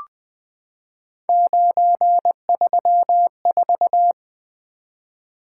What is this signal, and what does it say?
Morse code, heard as a keyed tone, about 20 wpm, 700 Hz. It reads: 934
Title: Three Letter-Number Combinations - 20wpm